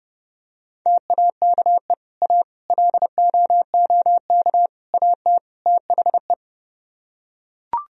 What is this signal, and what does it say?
Morse code, heard as a keyed tone, about 30 wpm, 700 Hz. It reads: TAKE A LOOK AT THE